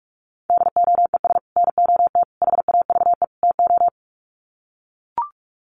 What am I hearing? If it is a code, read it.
Morse code, heard as a keyed tone, about 44 wpm, 700 Hz. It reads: DOES NOT HAVE TO
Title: Sets of 4 Words 45wpm